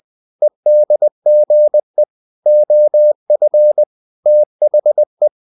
Morse code, heard as a keyed tone, about 20 wpm, 600 Hz. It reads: EDGE OF THE
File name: the old man and the sea / the old man and the sea part 1 20wpm